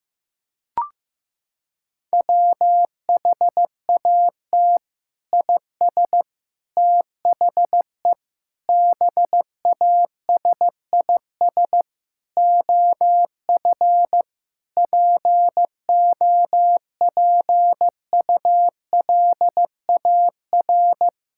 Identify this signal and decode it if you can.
Morse code, heard as a keyed tone, about 15 wpm, 700 Hz. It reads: WHAT IS THE BASIS OF POPULAR
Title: Sentences from Top 1000 Words - Review Part 5 - 15wpm